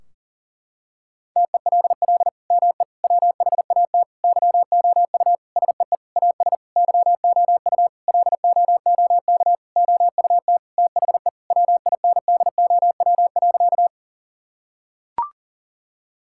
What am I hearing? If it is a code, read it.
Morse code, heard as a keyed tone, about 40 wpm, 700 Hz. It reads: TELL ME WHAT YOU SEE AS YOU LOOK OUT THE WINDOW.
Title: Sentences from Top 1600 Words 40wpm